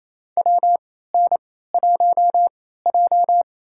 Morse code, heard as a keyed tone, about 28 wpm, spaced as for 16 wpm, 700 Hz. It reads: WN1J
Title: USA Call Signs - 4 Characters - 3x Character Spacing - 28wpm